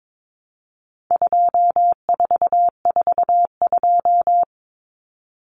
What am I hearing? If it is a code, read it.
Morse code, heard as a keyed tone, about 22 wpm, 700 Hz. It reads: 2442
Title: Numbers - 4-Digit 22wpm